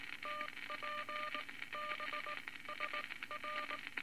Morse code, heard as a keyed tone, about 19 wpm, spaced as for 16 wpm, 1200 Hz, with noise. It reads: TPBSR
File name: Morse Code Records / 19 B LCM 167 Twelve Words Per Minute